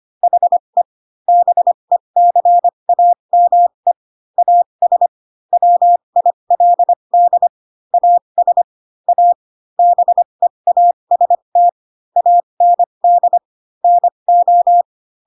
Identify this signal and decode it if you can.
Morse code, heard as a keyed tone, about 25 wpm, spaced as for 22 wpm, 700 Hz. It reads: HE BECAME AS WILD AS A BEAST AND NO
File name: Pinocchio / Pinocchio ch2 18 25 ik1hge